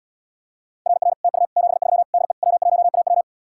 Morse code, heard as a keyed tone, about 80 wpm, 700 Hz. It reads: RR TU 73 DE W1AW